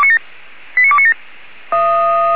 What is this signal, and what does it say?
Five-tone selective call (ZVEI), heard as quick runs of five tones, tones of about 70 ms. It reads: 89297